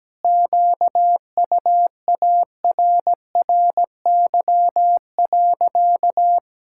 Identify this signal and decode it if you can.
Morse code, heard as a keyed tone, about 17 wpm, 700 Hz. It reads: QUARRY.
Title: Pangram Challenge v2 - 17wpm